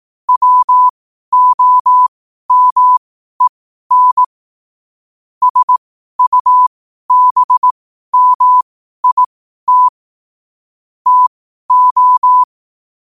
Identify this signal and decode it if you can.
Morse code, heard as a keyed tone, about 18 wpm, spaced as for 12 wpm, 1000 Hz. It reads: WOMEN SUBMIT TO
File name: 12wpm